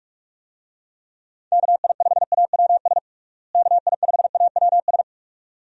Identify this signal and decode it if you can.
Morse code, heard as a keyed tone, about 46 wpm, 700 Hz. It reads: KI5AWS KI5AWS
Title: USA Call Signs - 6 Characters - Repeated 2x - 45wpm